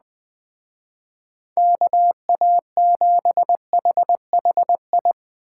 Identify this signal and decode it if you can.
Morse code, heard as a keyed tone, about 20 wpm, 700 Hz. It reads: KA7HHI
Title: Speed Racing - USA Call Signs 17wpm